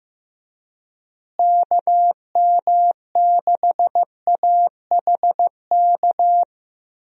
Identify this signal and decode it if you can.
Morse code, heard as a keyed tone, about 16 wpm, 700 Hz. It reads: KM6AHK
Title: USA Call Signs - 6 Characters - Repeated 3x - 15wpm